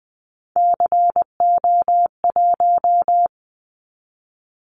Morse code, heard as a keyed tone, about 20 wpm, 700 Hz. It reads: CO1
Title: Callsign Prefixes for Top 100 DXCC - 20wpm